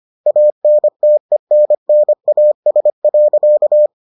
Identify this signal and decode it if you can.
Morse code, heard as a keyed tone, about 25 wpm, 600 Hz. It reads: ANTENNAS.